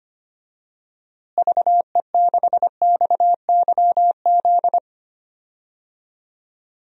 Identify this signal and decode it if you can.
Morse code, heard as a keyed tone, about 25 wpm, 700 Hz. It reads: VE6XYZ
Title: DXCC-100 Call Signs - 6 Characters - Repeated 3x - 25wpm